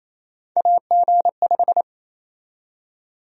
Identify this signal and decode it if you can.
Morse code, heard as a keyed tone, about 28 wpm, 700 Hz. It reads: AG5
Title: USA Call Sign Prefixes - Repeated 2x - 28wpm